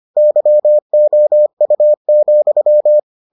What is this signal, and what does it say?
Morse code, heard as a keyed tone, about 25 wpm, 600 Hz. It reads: YOU,